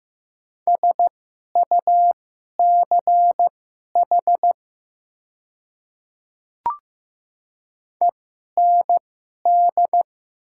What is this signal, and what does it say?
Morse code, heard as a keyed tone, about 15 wpm, spaced as for 11 wpm, 700 Hz. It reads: SUCH END
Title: 2x Character Spacing - Top 200 Words - 15wpm